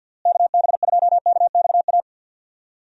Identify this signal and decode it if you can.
Morse code, heard as a keyed tone, about 46 wpm, 700 Hz. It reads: KB2KXA
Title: USA Call Signs - 6 Characters - Repeated 3x - 50wpm